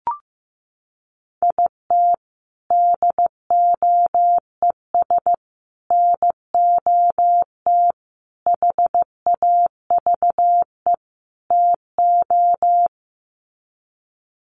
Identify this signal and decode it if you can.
Morse code, heard as a keyed tone, about 15 wpm, 700 Hz. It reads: IT DOES NOT HAVE TO
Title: Sets of 5 Words 15wpm